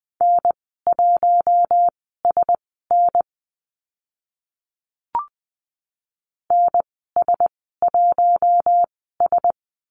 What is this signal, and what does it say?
Morse code, heard as a keyed tone, about 20 wpm, spaced as for 14 wpm, 700 Hz. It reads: N1SN NS1S
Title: 042 Lesson 8g - Number 1 - Call Signs 20wpm